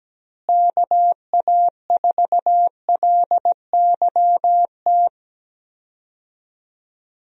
Morse code, heard as a keyed tone, about 17 wpm, 700 Hz. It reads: KA4LYT
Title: USA Call Signs - 6 Characters - Repeated 3x - 17wpm